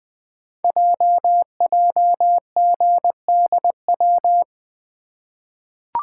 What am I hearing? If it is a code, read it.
Morse code, heard as a keyed tone, about 20 wpm, 700 Hz. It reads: JJGDW
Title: Code Groups - 5 Letters - 20wpm